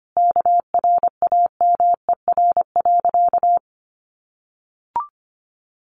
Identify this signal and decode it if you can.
Morse code, heard as a keyed tone, about 25 wpm, 700 Hz. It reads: KRAMER.
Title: A Sound Of Thunder 25wpm